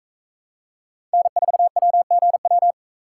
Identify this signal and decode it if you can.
Morse code, heard as a keyed tone, about 39 wpm, 700 Hz. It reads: N4WGW